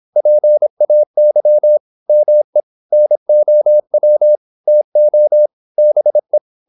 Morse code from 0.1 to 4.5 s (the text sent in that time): PAY ME NOW